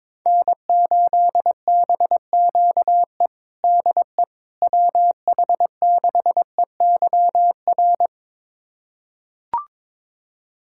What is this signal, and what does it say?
Morse code, heard as a keyed tone, about 22 wpm, 700 Hz. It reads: N8BQE DE WH6EYR
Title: USA Callsign DE Callsign 22wpm